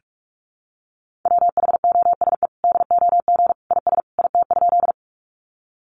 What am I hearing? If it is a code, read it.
Morse code, heard as a keyed tone, about 44 wpm, 700 Hz. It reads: WHOSE DOG IS IT?
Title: Sentences from Top 1200 Words 45wpm